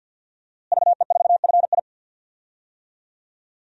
Morse code, heard as a keyed tone, about 52 wpm, 700 Hz. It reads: VE4FI
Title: DXCC-100 Call Signs - 5 Characters - Repeated 3x - 50wpm